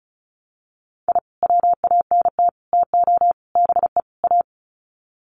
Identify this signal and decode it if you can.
Morse code, heard as a keyed tone, about 34 wpm, 700 Hz. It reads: I WANT TO BE A
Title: Sets of 5 Words Encore 35wpm